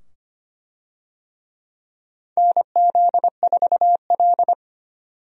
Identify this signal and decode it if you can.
Morse code, heard as a keyed tone, about 25 wpm, 700 Hz. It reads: NZ4L
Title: USA Call Signs - 4 Characters - 25wpm